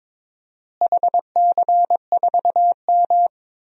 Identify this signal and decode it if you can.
Morse code, heard as a keyed tone, about 22 wpm, 700 Hz. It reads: HC4M
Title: DXCC-100 Call Signs - 4 Characters - Repeated 3x - 22wpm